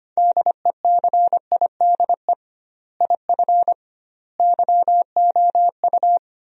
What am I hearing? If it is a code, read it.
Morse code, heard as a keyed tone, about 25 wpm, 700 Hz. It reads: DECIDE IF YOU
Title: 2x Word-Spacing - Sentences from Top 500 Words - 25wpm